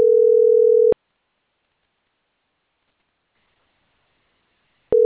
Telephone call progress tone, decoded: ringback tone